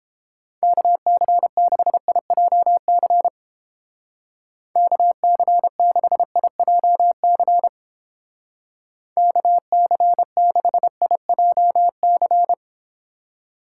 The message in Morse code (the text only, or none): KC6IJC KC6IJC KC6IJC